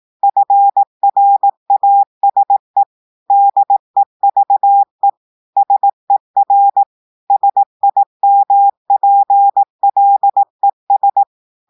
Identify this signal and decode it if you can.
Morse code, heard as a keyed tone, about 18 wpm, 800 Hz. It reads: FRASE DEVE SER SIMPLES
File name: audios / portugais / portugais texte en clair 2 vitesse 18 mots minutes